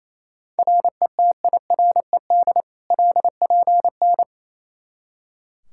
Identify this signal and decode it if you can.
Morse code, heard as a keyed tone, about 28 wpm, 700 Hz. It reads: RETIRED LPN
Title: Practice QSO - Exchange 3 - More Info - 28wpm